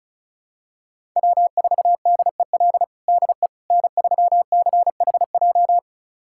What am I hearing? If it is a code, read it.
Morse code, heard as a keyed tone, about 36 wpm, 690 Hz. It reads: W4DEL DE N3CHJ
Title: USA Callsign DE Callsign - 5 Characters - 35wpm